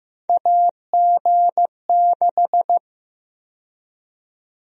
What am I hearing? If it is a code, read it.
Morse code, heard as a keyed tone, about 16 wpm, 700 Hz. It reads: AG6
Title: USA Call Sign Prefixes - 15wpm